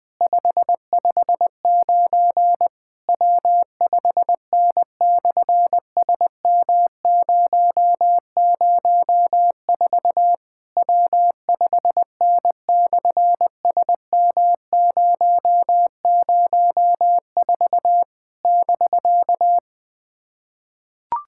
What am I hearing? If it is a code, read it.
Morse code, heard as a keyed tone, about 20 wpm, 700 Hz. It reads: 559 W5N/SM004 W5N/SM004 <BK>
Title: SOTA QSO Practice 20wpm